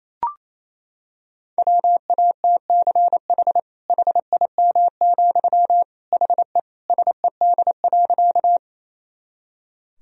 Morse code, heard as a keyed tone, about 28 wpm, 700 Hz. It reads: WATCH HIM, HE SED.
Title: Sentences from Top 300 Words and Abbreviations 28wpm